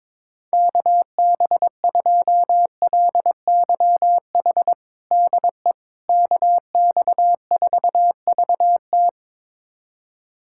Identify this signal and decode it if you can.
Morse code, heard as a keyed tone, about 22 wpm, 700 Hz. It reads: KB2LYH DE KX4VT
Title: USA Callsign DE Callsign 22wpm